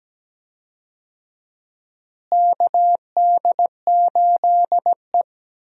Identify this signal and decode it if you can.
Morse code, heard as a keyed tone, about 17 wpm, 700 Hz. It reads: KD8E
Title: USA Call Signs - 4 Characters - Repeated 2x - 17wpm